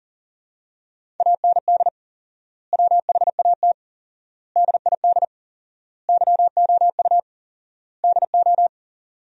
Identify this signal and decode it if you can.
Morse code, heard as a keyed tone, about 37 wpm, 700 Hz. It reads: AND WHAT DID YOU DO
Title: 4x Word-Spacing - Sentences from Top 300 Words - 40wpm